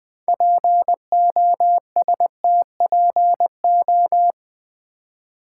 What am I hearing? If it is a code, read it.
Morse code, heard as a keyed tone, about 20 wpm, 700 Hz. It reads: POSTPO
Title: Build A Word - 2 Letters At A Time - 8-Letter Words - 20wpm